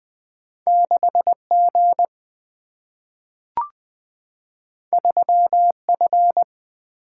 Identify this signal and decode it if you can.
Morse code, heard as a keyed tone, about 20 wpm, 700 Hz. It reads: 6G 3F
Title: Single-Letter Single-Number 20wpm